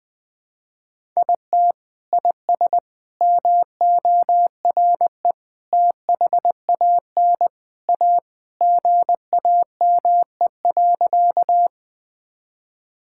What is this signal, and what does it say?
Morse code, heard as a keyed tone, about 20 wpm, 700 Hz. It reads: IT IS MORE THAN A GAME.
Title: Sentences with 3-Word Phrases - 20wpm